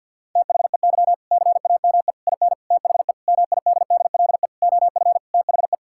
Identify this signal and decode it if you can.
Morse code, heard as a keyed tone, about 50 wpm, 700 Hz. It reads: THEY CAME IN THE MIDDLE OF THE
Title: Sentences from Top 1000 Words - Review Part 6 - 50wpm